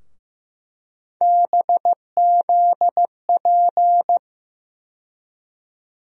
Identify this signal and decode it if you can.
Morse code, heard as a keyed tone, about 15 wpm, 700 Hz. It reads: BZP
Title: USA Call Sign Postfixes - 3 Letters 15wpm